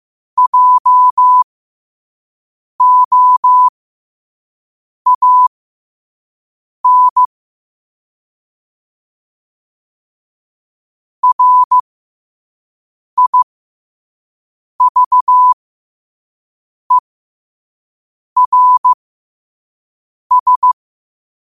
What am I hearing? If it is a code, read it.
Morse code, heard as a keyed tone, about 15 wpm, spaced as for 5 wpm, 1000 Hz. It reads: JOAN RIVERS
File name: qotdfiles / 5wpm